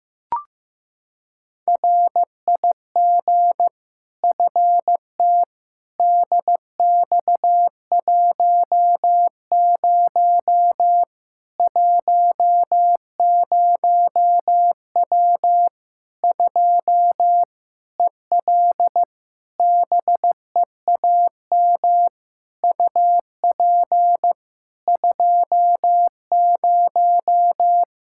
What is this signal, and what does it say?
Morse code, heard as a keyed tone, about 15 wpm, 700 Hz. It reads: RIG FTDX10 10W 2 EL BEAM UP 20
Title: Practice QSO - Exchange 2 - Info - 15wpm